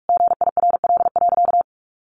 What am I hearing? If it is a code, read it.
Morse code, heard as a keyed tone, about 44 wpm, 700 Hz. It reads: GIRL.
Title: Sentences from Top 1100 Words 45wpm